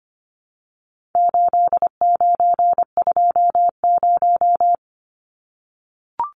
Morse code, heard as a keyed tone, about 25 wpm, 700 Hz. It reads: 8920
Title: Numbers - 4-Digit 25wpm